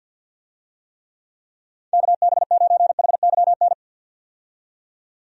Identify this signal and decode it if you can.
Morse code, heard as a keyed tone, about 52 wpm, 700 Hz. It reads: KB9HYN
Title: USA Call Signs - 6 Characters - 50wpm